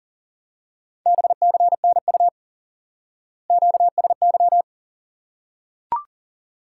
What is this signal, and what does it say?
Morse code, heard as a keyed tone, about 37 wpm, 700 Hz. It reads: BCNU QSY
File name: IQE Course 3 - Lesson 89 - YAGI - Two Elements - 40wpm